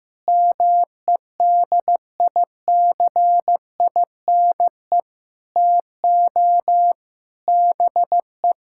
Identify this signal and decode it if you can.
Morse code, heard as a keyed tone, about 16 wpm, 700 Hz. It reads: MEDICINE TO BE
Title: Sentences from Top 1000 Words - Review Part 6 - 15wpm